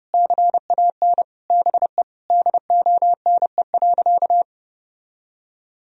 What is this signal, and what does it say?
Morse code, heard as a keyed tone, about 30 wpm, 700 Hz. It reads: CAN BE DONE.
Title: Sentences from Top 1000 Words - Review Part 1 - 30wpm